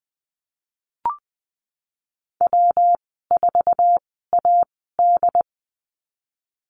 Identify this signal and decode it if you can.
Morse code, heard as a keyed tone, about 20 wpm, spaced as for 15 wpm, 700 Hz. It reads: W4AD
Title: 147 Lesson 24g - Letter y - Call Signs 20wpm